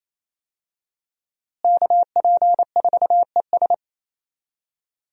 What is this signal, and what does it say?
Morse code, heard as a keyed tone, about 28 wpm, 700 Hz. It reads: KP4ES
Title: USA Call Signs - 5 Characters - Repeated 3x - 28wpm